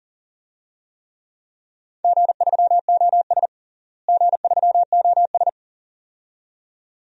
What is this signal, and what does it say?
Morse code, heard as a keyed tone, about 40 wpm, 690 Hz. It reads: G3OS G3OS
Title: DXCC-100 Call Signs - 4 Characters - Repeated 2x - 40wpm